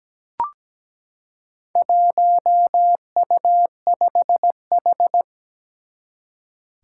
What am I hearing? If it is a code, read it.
Morse code, heard as a keyed tone, about 17 wpm, 700 Hz. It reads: 1U5H
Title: Four Challenging Letter-Number Combinations 17wpm